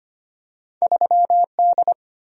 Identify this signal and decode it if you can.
Morse code, heard as a keyed tone, about 25 wpm, 700 Hz. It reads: 3D
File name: Two Letter-Number Combinations - 25wpm